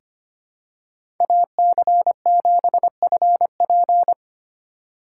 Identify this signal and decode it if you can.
Morse code, heard as a keyed tone, about 25 wpm, 700 Hz. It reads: AC7FP
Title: USA Call Signs - 5 Characters - 4x Character Spacing - 25wpm